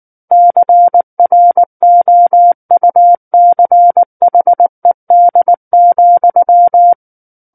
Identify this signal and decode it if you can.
Morse code, heard as a keyed tone, about 19 wpm, 700 Hz. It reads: CROUCHED,
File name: Book One / 10-Chapter 10 - 19 WPM